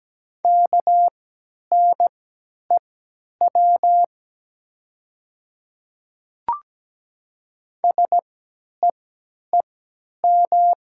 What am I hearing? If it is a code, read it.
Morse code, heard as a keyed tone, about 17 wpm, spaced as for 10 wpm, 700 Hz. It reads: KNEW SEEM
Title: 3x Character Spacing - Top 500 Words - 17wpm